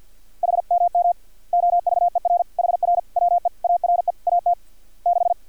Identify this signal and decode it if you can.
Morse code, heard as a keyed tone, about 50 wpm, 700 Hz. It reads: HMM OVER HR WE ARE AT 6
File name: archived / CRSnet